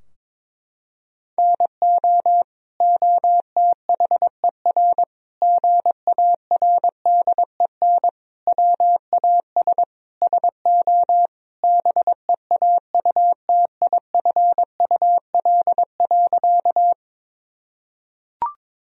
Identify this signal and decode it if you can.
Morse code, heard as a keyed tone, about 22 wpm, 700 Hz. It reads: NO OTHER GARDEN WAS SO BEAUTIFUL.
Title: Sentences from Top 1000 Words - Review Part 4 - 22wpm